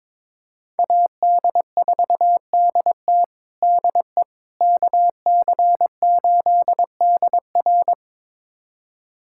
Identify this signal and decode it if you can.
Morse code, heard as a keyed tone, about 21 wpm, 700 Hz. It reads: AD4DT DE KC8DR
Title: USA Callsign DE Callsign - 5 Characters - 22wpm